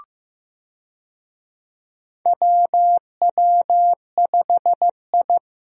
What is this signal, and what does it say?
Morse code, heard as a keyed tone, about 15 wpm, 700 Hz. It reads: WW5I
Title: USA Call Signs - 4 Characters - Repeated 3x - 15wpm